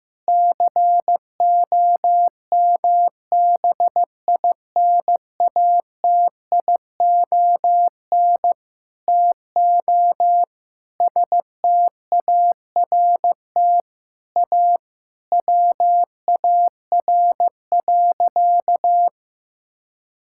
Morse code, heard as a keyed tone, about 15 wpm, 700 Hz. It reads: COMBINATION TO START A WAR.
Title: Sentences from Top 1000 Words - Review Part 6 - 15wpm